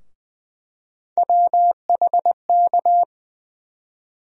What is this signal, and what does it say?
Morse code, heard as a keyed tone, about 20 wpm, 700 Hz. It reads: WHK
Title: USA Call Sign Postfixes - 3 Letters 20wpm